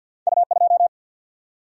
Morse code, heard as a keyed tone, about 52 wpm, 700 Hz. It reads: U2
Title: Single-Letter Single-Number 50wpm